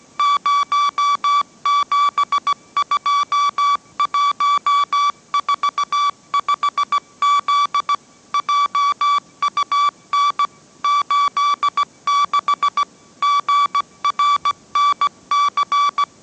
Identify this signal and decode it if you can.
Morse code, heard as a keyed tone, about 17 wpm, 1150 Hz. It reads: 072145Z JUN 86 GRNC
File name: mrx / mrx080 18wpm